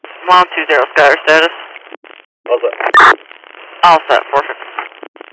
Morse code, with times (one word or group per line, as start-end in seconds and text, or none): none